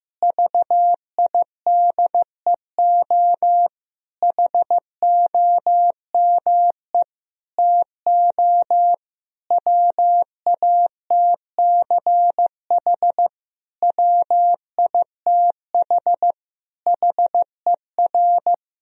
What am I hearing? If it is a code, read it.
Morse code, heard as a keyed tone, about 15 wpm, 700 Hz. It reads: VIDEO HOME TO WATCH WITH HER
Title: Sentences from Top 1000 Words - Review Part 4 - 15wpm